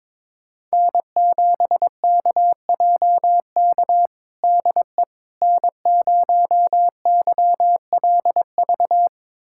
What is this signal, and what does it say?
Morse code, heard as a keyed tone, about 22 wpm, 700 Hz. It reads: N7KJK DE N0YLV
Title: USA Callsign DE Callsign - 5 Characters - 22wpm